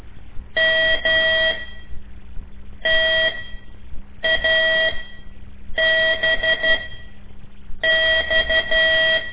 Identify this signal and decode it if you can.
Morse code, heard as a keyed tone, about 12 wpm, spaced as for 6 wpm, 1950 Hz. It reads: MTABX
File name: ik1weg2 / CORSI CW / lez5-1